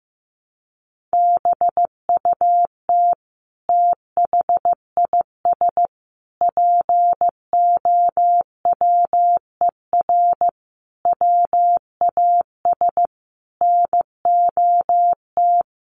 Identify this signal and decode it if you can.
Morse code, heard as a keyed tone, about 15 wpm, 700 Hz. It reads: BUT THIS POWER WAS NOT
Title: Sentences from Top 1500 Words 15wpm